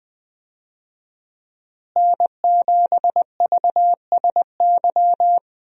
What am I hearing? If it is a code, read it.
Morse code, heard as a keyed tone, about 20 wpm, 700 Hz. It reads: N7VSY
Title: USA Call Signs - 5 Characters - Repeated 3x - 20wpm